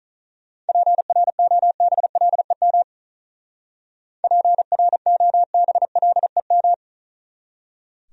Morse code, heard as a keyed tone, about 44 wpm, 700 Hz. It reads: PROBLEM PROBLEM